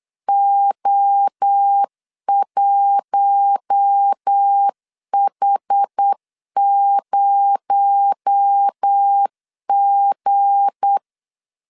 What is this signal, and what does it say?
Morse code, heard as a keyed tone, about 8 wpm, 800 Hz. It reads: O1H0G
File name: LEZIONI / 8 YJ45 / 8 YJ4 8wpm 3246